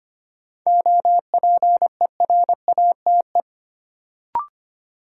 Morse code, heard as a keyed tone, about 25 wpm, 700 Hz. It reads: OPERATE
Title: Sentences from Top 1500 Words - New Word Drill - 25wpm